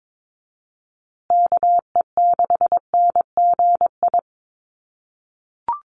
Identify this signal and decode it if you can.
Morse code, heard as a keyed tone, about 22 wpm, 700 Hz. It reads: KE6NGI
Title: USA Call Signs Repeated 2x 22wpm